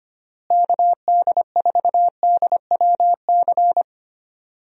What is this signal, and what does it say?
Morse code, heard as a keyed tone, about 25 wpm, 700 Hz. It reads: KD4DWC
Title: USA Call Signs - 6 Characters - Repeated 3x - 25wpm